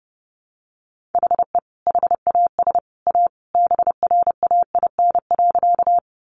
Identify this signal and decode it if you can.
Morse code, heard as a keyed tone, about 30 wpm, 700 Hz. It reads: HE HAS A BRAIN.
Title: Sentences from Top 1100 Words 30wpm